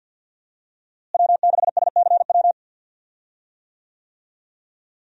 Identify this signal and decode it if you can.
Morse code, heard as a keyed tone, about 53 wpm, 700 Hz. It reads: W6SCW
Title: USA Call Signs - 5 Characters - Repeated 2x - 50wpm